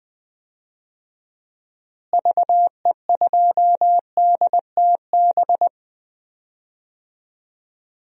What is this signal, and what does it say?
Morse code, heard as a keyed tone, about 20 wpm, 700 Hz. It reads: VE2DTB
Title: DXCC-100 Call Signs - 6 Characters - Repeated 3x - 20wpm